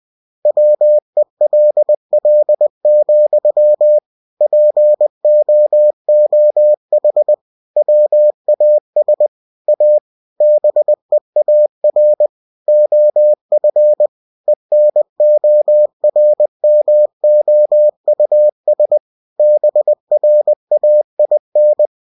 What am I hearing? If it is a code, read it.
Morse code, heard as a keyed tone, about 20 wpm, 600 Hz. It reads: WELL, POOH WAS A BEAR OF ENORMOUS BRAIN
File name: cw-books / winnie-the-pooh / winnie-20wpm-page-0104